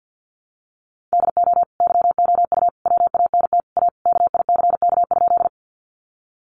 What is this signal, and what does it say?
Morse code, heard as a keyed tone, about 50 wpm, 700 Hz. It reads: DO YOU WANT A KICK?
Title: Sentences from Top 1300 Words 50wpm